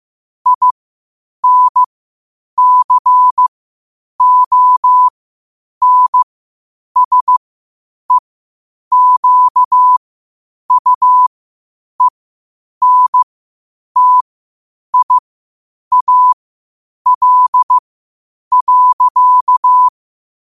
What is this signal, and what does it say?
Morse code, heard as a keyed tone, about 15 wpm, spaced as for 8 wpm, 1000 Hz. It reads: INCONSEQUENTIAL.
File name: qotdfiles / 8wpm